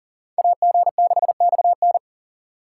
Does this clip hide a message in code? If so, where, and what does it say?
Yes: Morse code from 0.4 to 2.0 s, heard as a keyed tone, about 51 wpm, 700 Hz. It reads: AG6XN